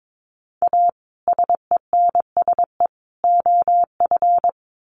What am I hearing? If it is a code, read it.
Morse code, heard as a keyed tone, about 22 wpm, 700 Hz. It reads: A SENSE OF
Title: Sets of 3 Words 22wpm